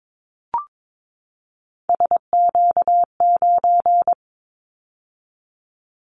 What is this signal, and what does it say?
Morse code, heard as a keyed tone, about 22 wpm, 700 Hz. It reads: SQ9
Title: Callsign Prefixes for Top 25 DXCC - 22wpm